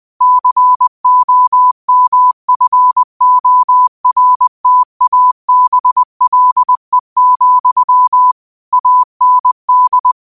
Morse code, heard as a keyed tone, about 20 wpm, 1000 Hz. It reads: COMFORTABLE, AND
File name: qotdfiles / 20wpm